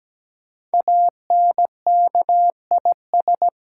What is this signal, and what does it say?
Morse code, heard as a keyed tone, about 17 wpm, 700 Hz. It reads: ANKIS
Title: Code Groups - 5 Letters - 17wpm